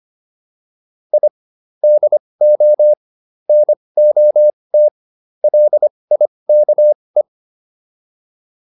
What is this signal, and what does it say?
Morse code, heard as a keyed tone, about 25 wpm, spaced as for 20 wpm, 600 Hz. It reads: I DO NOT LIKE